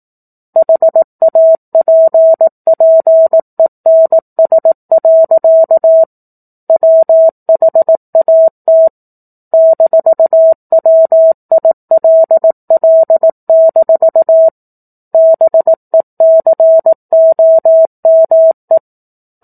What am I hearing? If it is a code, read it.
Morse code, heard as a keyed tone, about 18 wpm, 650 Hz. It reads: HAPPENS. WHAT -WILL- BECOME
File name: 04-Chapter-04 18 WPM